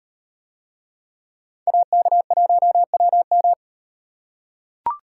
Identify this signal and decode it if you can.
Morse code, heard as a keyed tone, about 38 wpm, 700 Hz. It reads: AK1WM